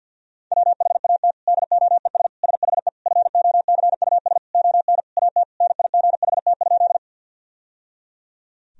Morse code, heard as a keyed tone, about 50 wpm, 700 Hz. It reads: WHAT DOES SHE FOCUS ON AT NIGHT?
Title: Sentences from Top 1000 Words - Review Part 4 - 50wpm